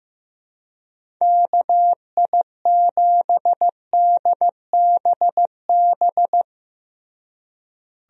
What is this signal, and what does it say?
Morse code, heard as a keyed tone, about 15 wpm, 700 Hz. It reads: KI7DBB
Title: USA Call Signs - 6 Characters - Repeated 2x - 15wpm